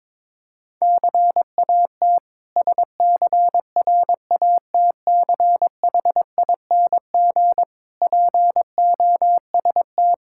Morse code, heard as a keyed tone, about 22 wpm, 700 Hz. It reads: CAT SCRATCHING POST
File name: Difficult Words with Hints - 22wpm